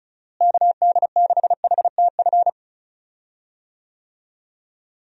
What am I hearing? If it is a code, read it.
Morse code, heard as a keyed tone, about 36 wpm, 700 Hz. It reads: KD6HTF
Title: USA Call Signs - 6 Characters - Repeated 2x - 35wpm